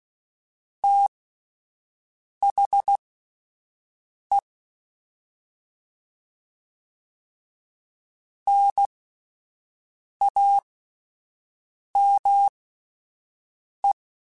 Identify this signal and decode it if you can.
Morse code, heard as a keyed tone, about 16 wpm, spaced as for 5 wpm, 800 Hz. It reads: THE NAME
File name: (spaced) / QSOs Spaced-mp3s / WD0BGP 1 @10 sp